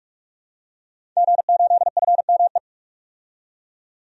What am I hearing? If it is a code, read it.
Morse code, heard as a keyed tone, about 42 wpm, 700 Hz. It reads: G8FME
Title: DXCC-100 Call Signs - 5 Characters - 45wpm